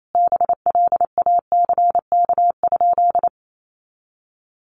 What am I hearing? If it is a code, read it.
Morse code, heard as a keyed tone, about 28 wpm, 700 Hz. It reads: BLACK?
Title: Sentences from Top 400 Words 28wpm